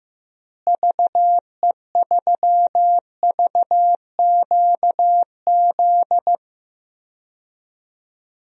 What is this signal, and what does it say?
Morse code, heard as a keyed tone, about 16 wpm, 700 Hz. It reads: VE3VQZ
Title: DXCC-100 Call Signs - 6 Characters - Repeated 2x - 15wpm